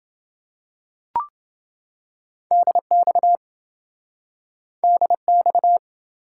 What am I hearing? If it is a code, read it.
Morse code, heard as a keyed tone, about 27 wpm, 700 Hz. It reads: DX DX